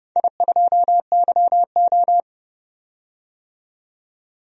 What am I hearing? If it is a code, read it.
Morse code, heard as a keyed tone, about 30 wpm, 700 Hz. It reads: I2YO
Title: DXCC-100 Call Signs - 4 Characters - 30wpm